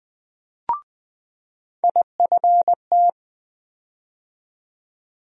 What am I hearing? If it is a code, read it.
Morse code, heard as a keyed tone, about 20 wpm, 700 Hz. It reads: IFT
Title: USA Call Sign Postfixes - 3 Letters 20wpm